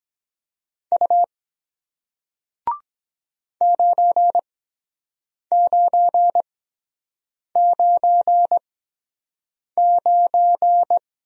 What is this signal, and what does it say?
Morse code, heard as a keyed tone, about 23 wpm, spaced as for 7 wpm, 700 Hz. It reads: U 9999